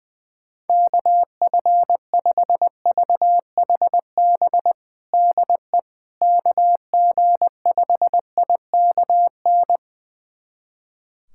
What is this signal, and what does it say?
Morse code, heard as a keyed tone, about 20 wpm, 700 Hz. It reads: KF5VHB DE KG5IKN
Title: USA Callsign DE Callsign - 6 Characters - 20wpm